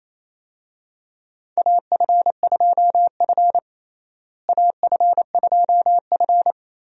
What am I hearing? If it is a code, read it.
Morse code, heard as a keyed tone, about 28 wpm, 700 Hz. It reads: AF2F AF2F
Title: USA Call Signs - 4 Characters - Repeated 3x - 28wpm